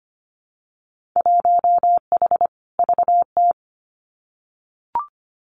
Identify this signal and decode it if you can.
Morse code, heard as a keyed tone, about 25 wpm, 700 Hz. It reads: 1H VT
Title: Winter Field Day Exchanges 25wpm